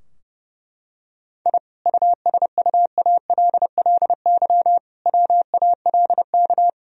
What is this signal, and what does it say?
Morse code, heard as a keyed tone, about 32 wpm, 700 Hz. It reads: I USUALLY WALK
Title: Sentences from Top 1000 Words - Review Part 3 - 30wpm